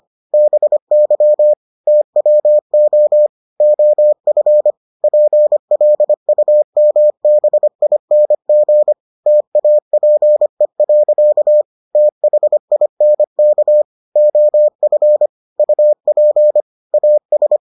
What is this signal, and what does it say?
Morse code, heard as a keyed tone, about 25 wpm, 600 Hz. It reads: BY TWO OF PLUMBING TAPE. THINK OF UP AS